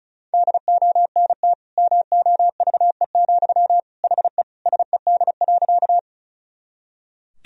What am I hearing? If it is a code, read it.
Morse code, heard as a keyed tone, about 36 wpm, 700 Hz. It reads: DONT MOVE, HE SED.
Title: Sentences from Top 300 Words and Abbreviations 35wpm